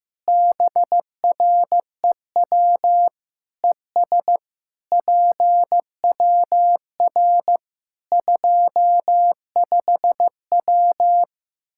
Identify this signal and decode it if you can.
Morse code, heard as a keyed tone, about 16 wpm, 700 Hz. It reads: BREW ES PWR 25W
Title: Practice QSO - Exchange 2 - Info - 15wpm